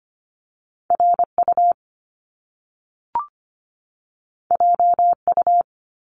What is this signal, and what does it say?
Morse code, heard as a keyed tone, about 25 wpm, 700 Hz. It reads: RU JU
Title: Top Two Letter Combinations in Words 25wpm